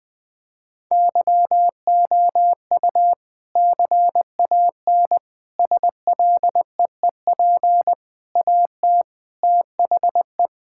Morse code, heard as a keyed tone, about 20 wpm, 700 Hz. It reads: YOU CAN SLEEP AT THE
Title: Sentences from Top 1600 Words 20wpm